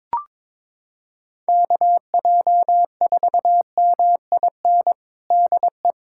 Morse code, heard as a keyed tone, about 22 wpm, 700 Hz. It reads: KJ4MIN DE
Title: USA Callsign DE Callsign - 6 Characters - 22wpm